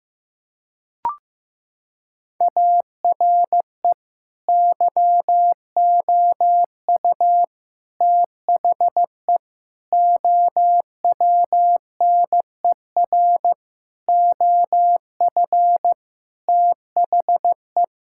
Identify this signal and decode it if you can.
Morse code, heard as a keyed tone, about 16 wpm, 700 Hz. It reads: ARE YOU THE OWNER OF THE
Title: Sentences from Top 1600 Words 15wpm